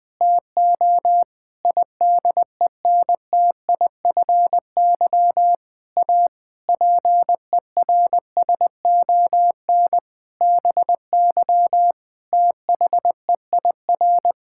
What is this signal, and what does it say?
Morse code, heard as a keyed tone, about 19 wpm, 700 Hz. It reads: TO IDENTIFY A PERSON BY THEIR
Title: Sentences from Top 1000 Words - Review Part 6 - 20wpm